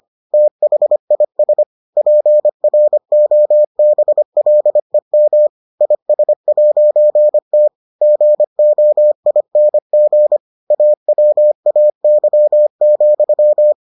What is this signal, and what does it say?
Morse code, heard as a keyed tone, about 25 wpm, 600 Hz. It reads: THIS PROBLEM IS'T GOING AWAY,